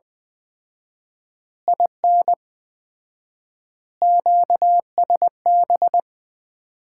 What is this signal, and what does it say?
Morse code, heard as a keyed tone, about 20 wpm, 700 Hz. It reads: IN QSB
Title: Common QSO Phrases - 4x Word-Spacing - 20wpm